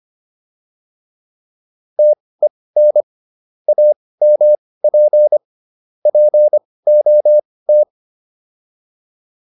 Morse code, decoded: TEN AMP POT

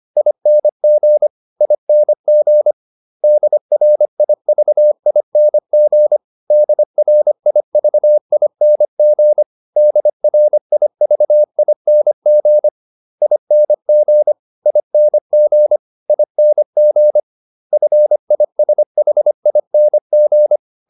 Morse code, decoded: ING ING DRIVING DRIVING DRIVING ING ING ING FISHING

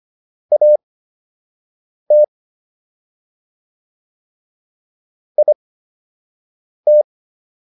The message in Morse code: AT IT

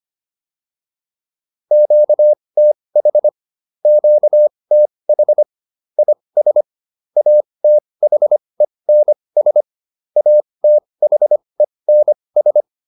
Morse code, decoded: QTH QTH IS ATHENS ATHENS

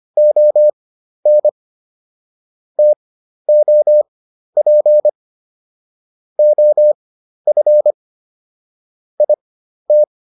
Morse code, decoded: ON TOP OF IT

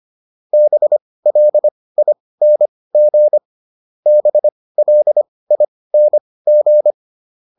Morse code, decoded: BLING BLING

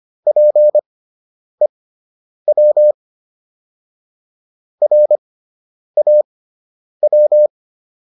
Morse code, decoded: PEW RAW

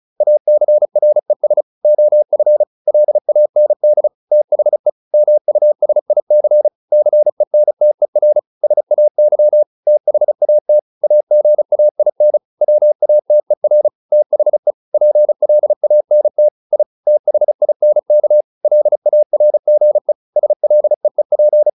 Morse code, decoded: ACRES OF LAND THE MUSIC CENTER SAY THAT AGAIN WATER THE PLANT I THINK LARGE SLEEP

